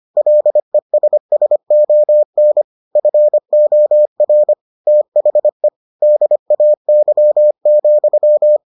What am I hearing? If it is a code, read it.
Morse code, heard as a keyed tone, about 25 wpm, 600 Hz. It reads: LESSON FOR THE DAY,